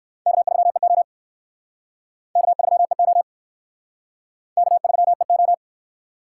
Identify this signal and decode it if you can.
Morse code, heard as a keyed tone, about 60 wpm, 700 Hz. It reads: K3EO K3EO K3EO